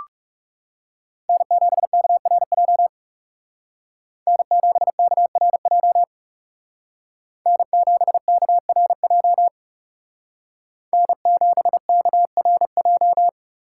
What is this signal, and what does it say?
Morse code, heard as a keyed tone, about 36 wpm, 700 Hz. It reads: N7KRJ N7KRJ N7KRJ N7KRJ
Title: Speed Racing - USA 1x3 Call Signs 30wpm